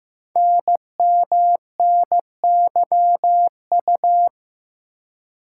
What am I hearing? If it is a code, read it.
Morse code, heard as a keyed tone, about 15 wpm, 700 Hz. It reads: NMNYU
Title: Code Groups - 5 Letters - 15wpm